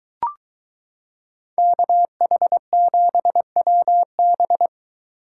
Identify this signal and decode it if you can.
Morse code, heard as a keyed tone, about 22 wpm, 700 Hz. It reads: KH7WB